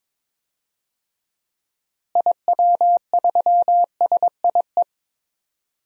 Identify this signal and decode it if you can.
Morse code, heard as a keyed tone, about 21 wpm, 700 Hz. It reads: IW3SIE
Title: DXCC-100 Call Signs - 6 Characters - Repeated 3x - 22wpm